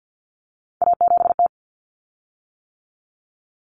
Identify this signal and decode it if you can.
Morse code, heard as a keyed tone, about 50 wpm, 700 Hz. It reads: A7T
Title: Common Cut Numbers - 3-Digit 50wpm